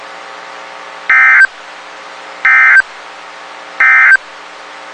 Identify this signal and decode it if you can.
Emergency Alert System (SAME) tones, heard as harsh buzzing bursts. It is heard from 1.0 to 4.3 s.
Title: WXK91 NOAA Weather Radio